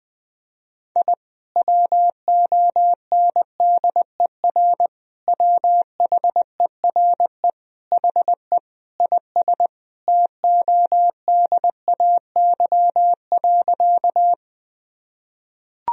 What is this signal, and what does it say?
Morse code, heard as a keyed tone, about 20 wpm, 700 Hz. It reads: I WONDER WHERE HE IS TODAY.
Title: Sentences from Top 1000 Words - Review Part 3 - 20wpm